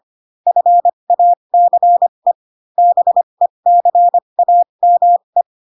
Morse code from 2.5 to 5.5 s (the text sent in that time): BECAME